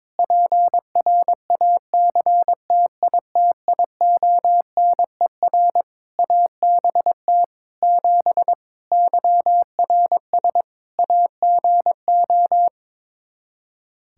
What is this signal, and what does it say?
Morse code, heard as a keyed tone, about 22 wpm, 700 Hz. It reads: PRACTITIONER ABT 7 YRS AGO
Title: Practice QSO - Exchange 3 - More Info - 22wpm